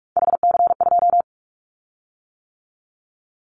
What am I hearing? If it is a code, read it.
Morse code, heard as a keyed tone, about 46 wpm, 700 Hz. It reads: HC2
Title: Callsign Prefixes for Top 100 DXCC - 45wpm